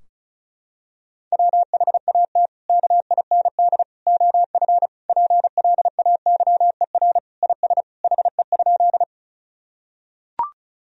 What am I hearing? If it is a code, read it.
Morse code, heard as a keyed tone, about 34 wpm, 700 Hz. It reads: WHAT KIND OF PLAYER IS HE?
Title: Sentences from Top 1000 Words - Review Part 4 - 35wpm